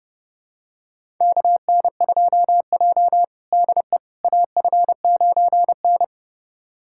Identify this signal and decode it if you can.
Morse code, heard as a keyed tone, about 32 wpm, 700 Hz. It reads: KN2J DE AF9N
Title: USA Callsign DE Callsign - 4 Characters - 30wpm